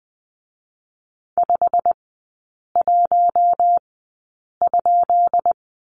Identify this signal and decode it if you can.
Morse code, heard as a keyed tone, about 20 wpm, 700 Hz. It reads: 5 1 ?
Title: 133 Lesson 22e - Question Mark - Three Characters 20wpm